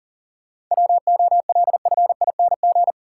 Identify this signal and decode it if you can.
Morse code, heard as a keyed tone, about 40 wpm, 700 Hz. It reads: WOLFING
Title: Pangram Challenge v2 - 40wpm